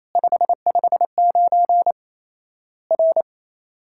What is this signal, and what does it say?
Morse code, heard as a keyed tone, about 28 wpm, 700 Hz. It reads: 559 R
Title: Worked All Europe DX Contest - QTC Exchange - 28wpm